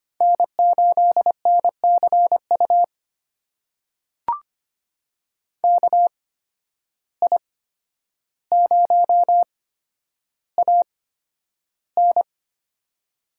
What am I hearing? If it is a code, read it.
Morse code, heard as a keyed tone, about 25 wpm, spaced as for 11 wpm, 700 Hz. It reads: N8NCU KI0AN
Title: USA Call Signs - 5 Characters - 8x Character Spacing - 25wpm